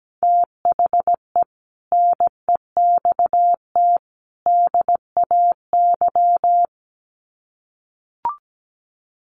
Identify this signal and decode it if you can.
Morse code, heard as a keyed tone, about 17 wpm, 700 Hz. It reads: THE NEXT DAY
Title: Sets of 3 Words 17wpm